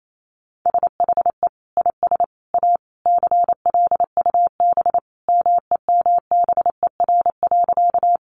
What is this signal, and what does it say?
Morse code, heard as a keyed tone, about 28 wpm, 700 Hz. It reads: SHE IS A CLUB MEMBER.
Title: Sentences from Top 1200 Words 28wpm